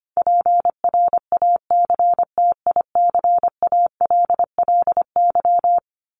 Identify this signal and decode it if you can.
Morse code, heard as a keyed tone, about 25 wpm, 700 Hz. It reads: PRACTICALLY
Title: LY Suffix - Top 100 Words with Base Words 25wpm